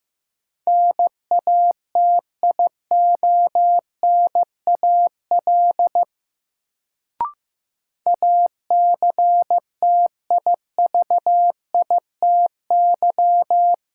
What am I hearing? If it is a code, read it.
Morse code, heard as a keyed tone, about 15 wpm, 700 Hz. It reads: NATIONAL ACTIVITY
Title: Sentences from Top 1000 Words - Review Part 4 - New Word Drill - 15wpm